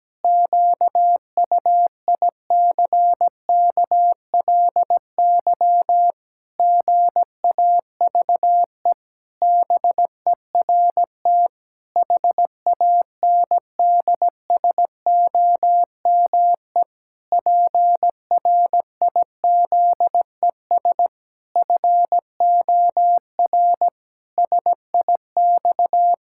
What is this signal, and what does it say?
Morse code, heard as a keyed tone, about 17 wpm, 700 Hz. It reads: QUICKLY GAVE BERT HANDSOME PRIZES FOR SIX
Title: Pangram Challenge v2 - 17wpm